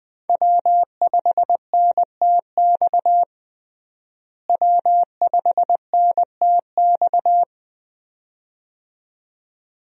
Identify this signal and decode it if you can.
Morse code, heard as a keyed tone, about 21 wpm, 700 Hz. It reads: W5NTX W5NTX
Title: USA Call Signs - 5 Characters - Repeated 2x - 20wpm